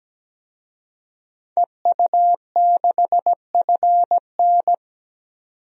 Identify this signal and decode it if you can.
Morse code, heard as a keyed tone, about 17 wpm, 700 Hz. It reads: EU6FN
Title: DXCC-100 Call Signs - 5 Characters - 17wpm